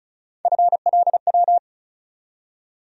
Morse code, heard as a keyed tone, about 34 wpm, 700 Hz. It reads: FLW
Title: USA Call Sign Postfixes - 3 Letters 35wpm